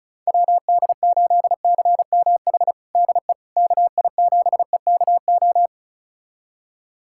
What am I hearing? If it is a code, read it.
Morse code, heard as a keyed tone, about 36 wpm, 700 Hz. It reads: WD8CMH DE KI7EKO
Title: USA Callsign DE Callsign 35wpm